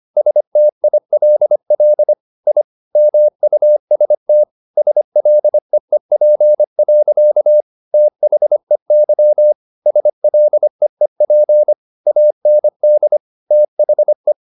Morse code, heard as a keyed tone, about 25 wpm, 600 Hz. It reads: STILL I MUST SLEEP. THEY SLEEP AND THE